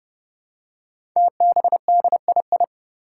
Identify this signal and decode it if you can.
Morse code, heard as a keyed tone, about 30 wpm, 700 Hz. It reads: TBDII
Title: Five Challenging Letter-Number Combinations 30wpm